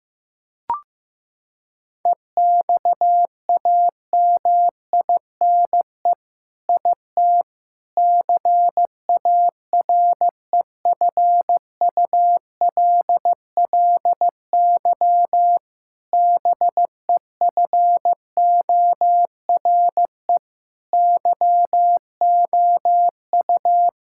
Morse code, heard as a keyed tone, about 15 wpm, 700 Hz. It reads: EXAMINE IT CAREFULLY BEFORE YOU
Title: Sentences from Top 1600 Words 15wpm